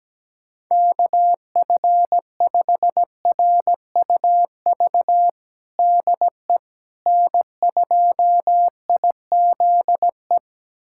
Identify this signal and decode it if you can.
Morse code, heard as a keyed tone, about 17 wpm, 700 Hz. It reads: KF5RUV DE N2IZE
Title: USA Callsign DE Callsign 17wpm